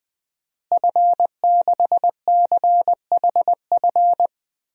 Morse code, heard as a keyed tone, about 20 wpm, 700 Hz. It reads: F6CHF
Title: DXCC-100 Call Signs - 5 Characters - Repeated 3x - 20wpm